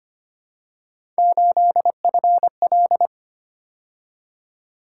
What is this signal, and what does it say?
Morse code, heard as a keyed tone, about 25 wpm, 700 Hz. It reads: 8FL
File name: Three Letter-Number Combinations - 25wpm